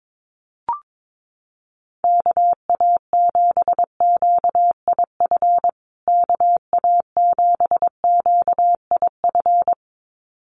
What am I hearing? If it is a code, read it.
Morse code, heard as a keyed tone, about 22 wpm, 700 Hz. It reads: KA7QIF KA7QIF
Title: USA Call Signs Repeated 2x 22wpm